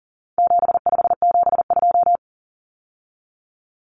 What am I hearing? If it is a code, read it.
Morse code, heard as a keyed tone, about 40 wpm, 700 Hz. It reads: 7572
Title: Numbers - 4-Digit 40wpm